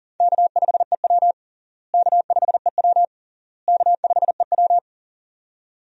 Morse code, heard as a keyed tone, about 40 wpm, 700 Hz. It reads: K5EW K5EW K5EW
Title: USA Call Signs - 4 Characters - Repeated 3x - 40wpm